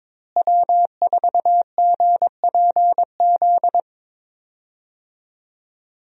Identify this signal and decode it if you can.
Morse code, heard as a keyed tone, about 22 wpm, 700 Hz. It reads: W4GPZ
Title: USA Call Signs - 5 Characters - Repeated 3x - 22wpm